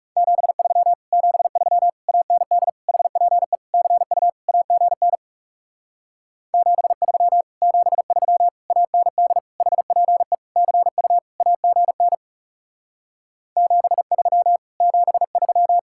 Morse code, heard as a keyed tone, about 40 wpm, 700 Hz. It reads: 73 73 AND HPE CU AGN 73 73 AND HPE CU AGN 73 73